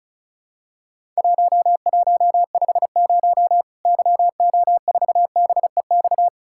Morse code, heard as a keyed tone, about 36 wpm, 700 Hz. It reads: 1150 YO4BEX
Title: Worked All Europe DX Contest - QTC Exchange - 35wpm